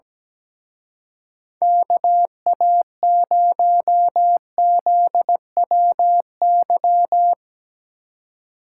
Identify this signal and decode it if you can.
Morse code, heard as a keyed tone, about 19 wpm, 700 Hz. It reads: KA0ZWY
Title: Speed Racing - USA Call Signs 17wpm